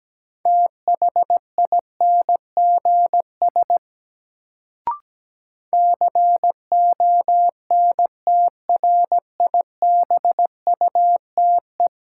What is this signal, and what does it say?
Morse code, heard as a keyed tone, about 17 wpm, 700 Hz. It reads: THINGS CONTRIBUTE
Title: Sentences from Top 1400 Words - New Word Drill Alt - 17wpm